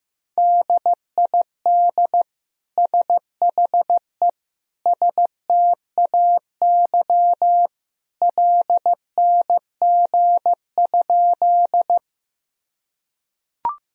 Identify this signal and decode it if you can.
Morse code, heard as a keyed tone, about 15 wpm, 700 Hz. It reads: DID SHE STAY LNG?
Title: Sentences from Top 500 Words and Abbreviations 15wpm